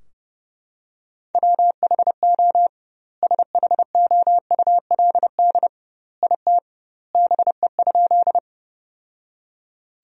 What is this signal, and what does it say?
Morse code, heard as a keyed tone, about 32 wpm, 700 Hz. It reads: WHO SHOULD IT BE?
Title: 2x Word-Spacing - Sentences from Top 300 Words - 30wpm